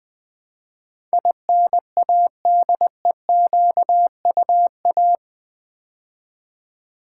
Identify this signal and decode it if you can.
Morse code, heard as a keyed tone, about 19 wpm, 700 Hz. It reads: INADEQUA
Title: Build A Word - 4 Letters At A Time - 10-Letter Words - 20wpm